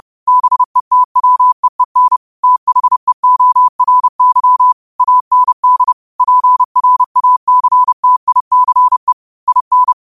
Morse code, heard as a keyed tone, about 30 wpm, 1000 Hz. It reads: BETWEEN THEORY AND PRACTICE IN